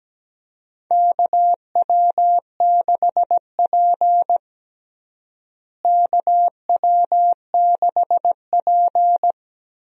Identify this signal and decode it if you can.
Morse code, heard as a keyed tone, about 17 wpm, 700 Hz. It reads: KW6P KW6P
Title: USA Call Signs - 4 Characters - Repeated 2x - 17wpm